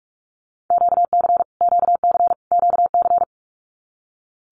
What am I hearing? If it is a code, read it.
Morse code, heard as a keyed tone, about 44 wpm, 700 Hz. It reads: QC QC QC
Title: Canadian Provinces and Territories - Repeated 3x - 45wpm